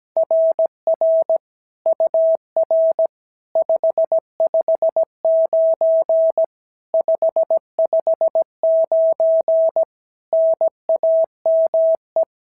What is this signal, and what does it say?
Morse code, heard as a keyed tone, about 17 wpm, 690 Hz. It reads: RR UR 559 559 NAME
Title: SOTA QSO Practice 17wpm